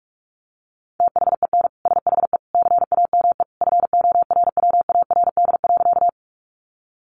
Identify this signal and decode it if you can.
Morse code, heard as a keyed tone, about 44 wpm, 700 Hz. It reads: THEN SHE CAME FORWARD.
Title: Sentences from Top 1300 Words 45wpm